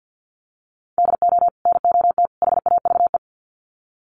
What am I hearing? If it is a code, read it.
Morse code, heard as a keyed tone, about 50 wpm, 700 Hz. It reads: DO NOT HAVE
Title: Sets of 3 Words 50wpm